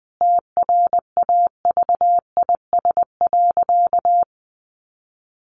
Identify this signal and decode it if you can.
Morse code, heard as a keyed tone, about 20 wpm, 700 Hz. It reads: TRAVIS.
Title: A Sound Of Thunder 20wpm